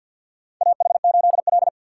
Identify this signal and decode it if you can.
Morse code, heard as a keyed tone, about 47 wpm, 700 Hz. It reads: AH8L
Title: USA Call Signs - 4 Characters - 2x Character Spacing - 50wpm